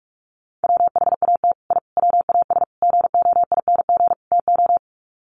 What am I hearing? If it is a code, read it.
Morse code, heard as a keyed tone, about 44 wpm, 700 Hz. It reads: WHAT I WAS GOING TO
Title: Sets of 5 Words Encore2 45wpm